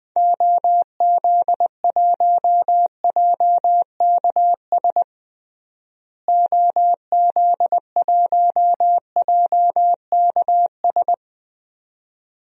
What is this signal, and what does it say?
Morse code, heard as a keyed tone, about 20 wpm, 700 Hz. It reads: OZ1JKS OZ1JKS
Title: DXCC-100 Call Signs - 6 Characters - Repeated 2x - 20wpm